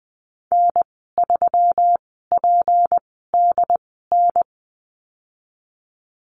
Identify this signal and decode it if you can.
Morse code, heard as a keyed tone, about 19 wpm, spaced as for 14 wpm, 700 Hz. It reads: N3PDN
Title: 167 Lesson 27g - Number 7 - Call Signs 20wpm